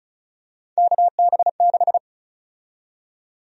Morse code, heard as a keyed tone, about 34 wpm, 700 Hz. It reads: KB6
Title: USA Call Sign Prefixes - Repeated 2x - 35wpm